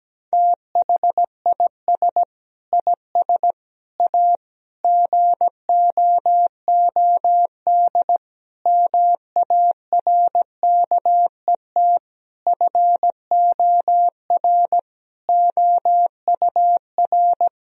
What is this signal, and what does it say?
Morse code, heard as a keyed tone, about 17 wpm, 700 Hz. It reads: THIS IS A GOOD MARKET FOR OUR
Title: Sentences from Top 1000 Words - Review Part 3 - 17wpm